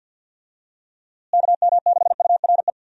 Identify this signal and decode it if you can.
Morse code, heard as a keyed tone, about 51 wpm, 700 Hz. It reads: KM6URE
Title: USA Call Signs - 6 Characters - Repeated 3x - 50wpm